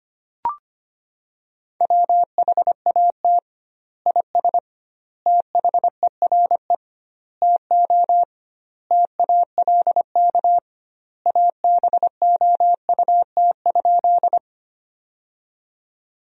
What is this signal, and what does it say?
Morse code, heard as a keyed tone, about 25 wpm, 700 Hz. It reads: WHAT IS THERE TO TALK ABOUT?
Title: 2x Word-Spacing - Sentences from Top 400 Words - 25wpm